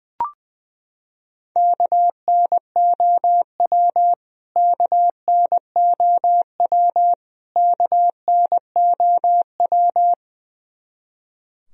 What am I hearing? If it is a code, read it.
Morse code, heard as a keyed tone, about 20 wpm, 700 Hz. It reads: KNOW KNOW KNOW
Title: Top 100 Words Repeated 20wpm